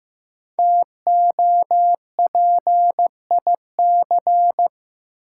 Morse code, heard as a keyed tone, about 15 wpm, 700 Hz. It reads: TOPIC
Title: Sentences from Top 1200 Words - New Word Drill Alt - 15wpm